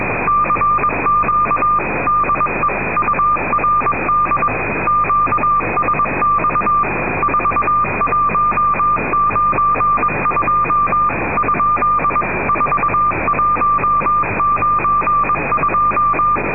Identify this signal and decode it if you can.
Morse code, heard as a keyed tone, about 22 wpm, 1190 Hz, with noise. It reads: CQ DE URD QSX 4192 ? 4192
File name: europe / rus / rus-4290-sankt petersburg radio urd-whe-1992